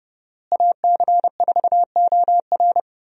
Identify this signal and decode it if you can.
Morse code, heard as a keyed tone, about 31 wpm, 700 Hz. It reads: AC4OR
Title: USA Call Signs - 5 Characters - Repeated 3x - 30wpm